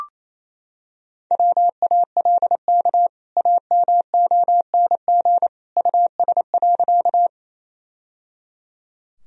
Morse code, heard as a keyed tone, about 27 wpm, 700 Hz. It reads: WALK AMONG US.
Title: Sentences from Top 1000 Words - Review Part 3 - 28wpm